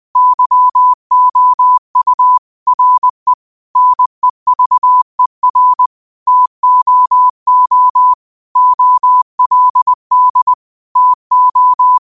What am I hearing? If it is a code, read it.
Morse code, heard as a keyed tone, about 20 wpm, 1000 Hz. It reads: YOU RE NEVER TOO OLD TO
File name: qotdfiles / 20wpm